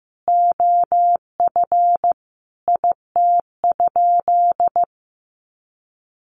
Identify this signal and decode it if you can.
Morse code, heard as a keyed tone, about 15 wpm, 700 Hz. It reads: OF IT?
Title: Sentences from Top 300 Words 15wpm